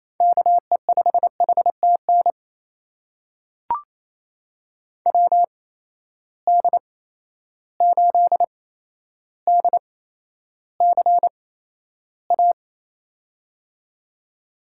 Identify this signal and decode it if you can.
Morse code, heard as a keyed tone, about 28 wpm, spaced as for 12 wpm, 700 Hz. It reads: KE5HTN WD8DCA
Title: USA Call Signs - 6 Characters - 8x Character Spacing - 28wpm